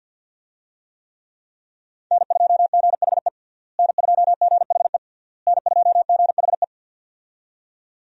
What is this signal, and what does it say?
Morse code, heard as a keyed tone, about 49 wpm, 700 Hz. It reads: N2GHE N2GHE N2GHE
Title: USA Call Signs - 5 Characters - Repeated 3x - 50wpm